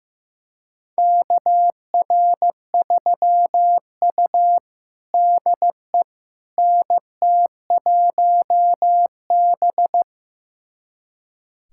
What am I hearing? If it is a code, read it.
Morse code, heard as a keyed tone, about 16 wpm, 700 Hz. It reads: KR3U DE NT1B
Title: USA Callsign DE Callsign - 4 Characters - 15wpm